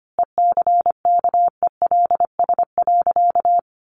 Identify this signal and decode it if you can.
Morse code, heard as a keyed tone, about 25 wpm, 700 Hz. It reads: ECKELS.
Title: A Sound Of Thunder 25wpm